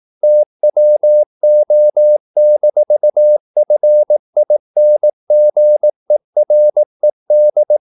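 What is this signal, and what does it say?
Morse code, heard as a keyed tone, about 18 wpm, 600 Hz. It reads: TWO-FINGERED
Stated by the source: Morse practice at 18 wpm, English text, from CW med Gnister.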